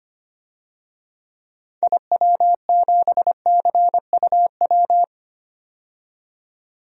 Morse code, heard as a keyed tone, about 25 wpm, 700 Hz. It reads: IW7CUW
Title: DXCC-100 Call Signs - 6 Characters - 25wpm